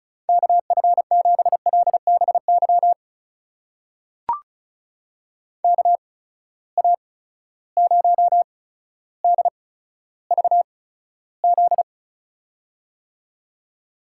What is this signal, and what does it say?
Morse code, heard as a keyed tone, about 36 wpm, spaced as for 15 wpm, 700 Hz. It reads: KF7LBY KA0DVZ
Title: USA Call Signs - 6 Characters - 8x Character Spacing - 35wpm